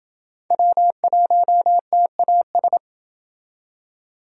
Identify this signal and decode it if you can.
Morse code, heard as a keyed tone, about 24 wpm, 700 Hz. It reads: W1TAS
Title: Speed Racing - USA 1x3 Call Signs 20wpm